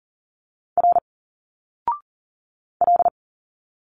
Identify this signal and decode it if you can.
Morse code, heard as a keyed tone, about 40 wpm, 700 Hz. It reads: R L
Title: Single Letters 40wpm